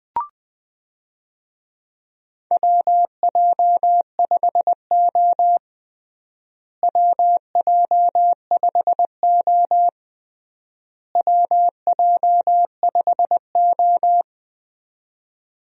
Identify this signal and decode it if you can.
Morse code, heard as a keyed tone, about 21 wpm, 700 Hz. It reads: WJ5O WJ5O WJ5O
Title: USA Call Signs - 4 Characters - Repeated 3x - 20wpm